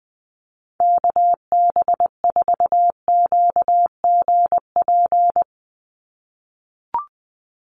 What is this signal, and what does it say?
Morse code, heard as a keyed tone, about 20 wpm, 700 Hz. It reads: KB4QGP
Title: USA Call Signs 20wpm